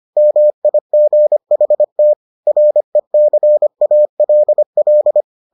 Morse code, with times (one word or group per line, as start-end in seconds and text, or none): 0.2-2.1 MIGHT
2.5-5.2 RECALL